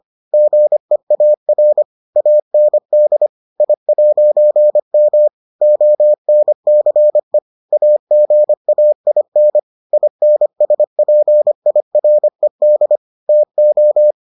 Morse code, heard as a keyed tone, about 25 wpm, 600 Hz. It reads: GEAR AND I'M ONCE AGAIN INSPIRED TO